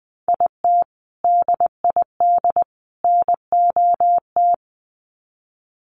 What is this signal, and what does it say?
Morse code, heard as a keyed tone, about 20 wpm, 700 Hz. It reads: IT DID NOT
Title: Sets of 3 Words 20wpm